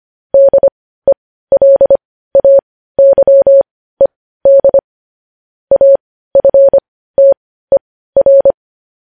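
Morse code, heard as a keyed tone, about 25 wpm, spaced as for 15 wpm, 550 Hz. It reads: DELAYED AFTER
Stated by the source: Morse code news headlines sent at 15, preceded by vvv vvv.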